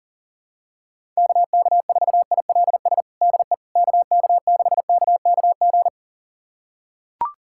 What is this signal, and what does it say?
Morse code, heard as a keyed tone, about 40 wpm, 700 Hz. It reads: KK4ILS DE KK6KKG
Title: USA Callsign DE Callsign - 6 Characters - 40wpm